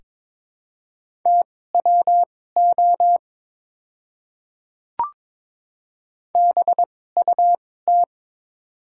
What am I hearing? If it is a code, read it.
Morse code, heard as a keyed tone, about 22 wpm, spaced as for 16 wpm, 700 Hz. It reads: TWO BUT
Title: 2x Character Spacing - Top 100 Words - 22wpm